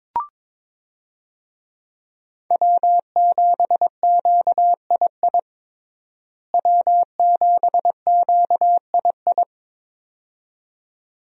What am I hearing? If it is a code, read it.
Morse code, heard as a keyed tone, about 22 wpm, 700 Hz. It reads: W7QII W7QII
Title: USA Call Signs - 5 Characters - Repeated 2x - 22wpm